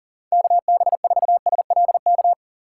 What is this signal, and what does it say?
Morse code, heard as a keyed tone, about 40 wpm, 690 Hz. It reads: KB4SLK
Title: USA Call Signs - 2x Character Spacing - 40wpm